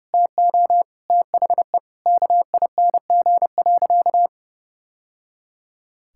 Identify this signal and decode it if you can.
Morse code, heard as a keyed tone, about 30 wpm, 700 Hz. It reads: TO THE KING.
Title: Sentences from Top 1000 Words - Review Part 6 - 30wpm